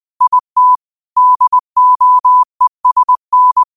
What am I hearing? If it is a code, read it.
Morse code, heard as a keyed tone, about 20 wpm, 1000 Hz. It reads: IT DOESN
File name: qotdfiles / 20wpm